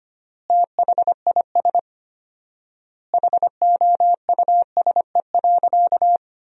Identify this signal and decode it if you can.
Morse code, heard as a keyed tone, about 25 wpm, 700 Hz. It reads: THIS HOUSE.
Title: 4x Word-Spacing - Sentences from Top 400 Words - 25wpm